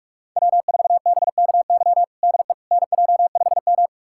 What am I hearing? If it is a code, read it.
Morse code, heard as a keyed tone, about 42 wpm, 700 Hz. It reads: W4BKY DE NJ5M
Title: USA Callsign DE Callsign 45wpm